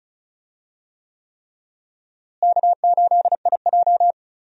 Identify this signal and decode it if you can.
Morse code, heard as a keyed tone, about 36 wpm, 700 Hz. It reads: K8IJ
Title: USA Call Signs - 4 Characters - Repeated 3x - 35wpm